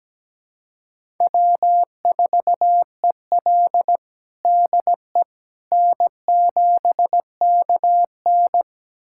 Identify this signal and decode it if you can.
Morse code, heard as a keyed tone, about 17 wpm, 700 Hz. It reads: W4EL DE N7KN
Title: USA Callsign DE Callsign - 4 Characters - 17wpm